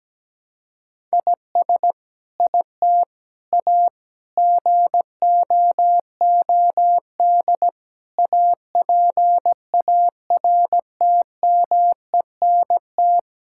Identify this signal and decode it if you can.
Morse code, heard as a keyed tone, about 17 wpm, 700 Hz. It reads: IS IT A GOOD APARTMENT
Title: Sentences from Top 1000 Words - Review Part 6 - 17wpm